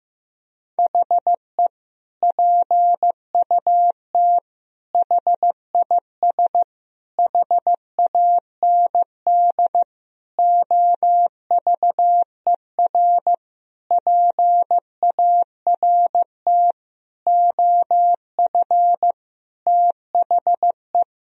Morse code, heard as a keyed tone, about 16 wpm, 700 Hz. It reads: HE PUT HIS HAND OVER PART OF THE
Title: Sentences from Top 1000 Words - Review Part 2 - 15wpm